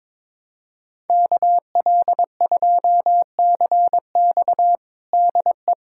Morse code, heard as a keyed tone, about 22 wpm, 700 Hz. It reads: KL2CX DE
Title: USA Callsign DE Callsign - 5 Characters - 22wpm